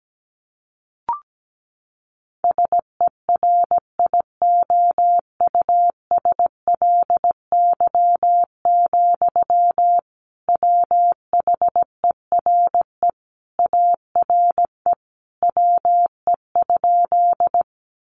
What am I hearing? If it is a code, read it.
Morse code, heard as a keyed tone, about 17 wpm, 700 Hz. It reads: SERIOUSLY, WHERE ARE WE?
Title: Sentences from Top 1200 Words 17wpm